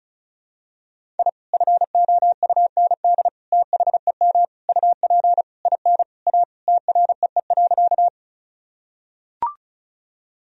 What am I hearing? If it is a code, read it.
Morse code, heard as a keyed tone, about 34 wpm, 700 Hz. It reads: I FOUND THEM UP IN A TREE.
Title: Sentences from Top 1000 Words - Review Part 1 - 35wpm